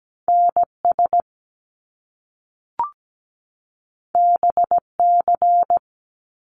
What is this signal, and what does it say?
Morse code, heard as a keyed tone, about 17 wpm, 700 Hz. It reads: NS BC
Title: Canadian Provinces and Territories 17wpm